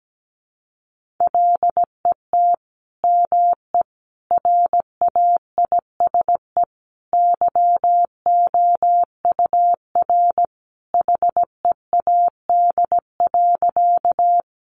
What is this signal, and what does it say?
Morse code, heard as a keyed tone, about 17 wpm, 700 Hz. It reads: LET ME RAISE YOUR HEAD.
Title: Sentences from Top 1400 Words 17wpm